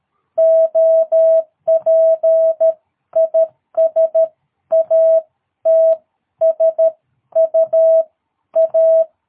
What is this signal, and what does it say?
Morse code, heard as a keyed tone, about 13 wpm, 660 Hz. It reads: OPIS AT S U A
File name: jh8cbh / 2022cw